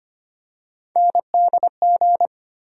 Morse code, heard as a keyed tone, about 25 wpm, 700 Hz. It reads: NDG
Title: USA Call Sign Postfixes - 3 Letters 25wpm